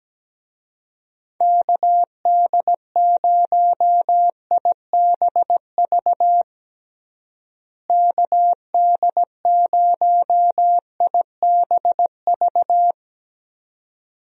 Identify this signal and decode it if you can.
Morse code, heard as a keyed tone, about 17 wpm, 700 Hz. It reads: KD0IBV KD0IBV
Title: USA Call Signs - 6 Characters - Repeated 2x - 17wpm